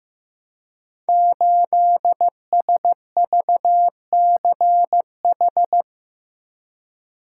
Morse code, heard as a keyed tone, about 15 wpm, 700 Hz. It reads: 8SVCH
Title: Code Groups - 5 Letter:Numbers - 15wpm